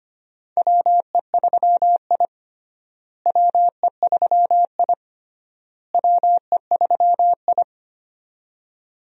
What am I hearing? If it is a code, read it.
Morse code, heard as a keyed tone, about 25 wpm, 700 Hz. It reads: WE3I WE3I WE3I
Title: USA Call Signs - 4 Characters - Repeated 3x - 25wpm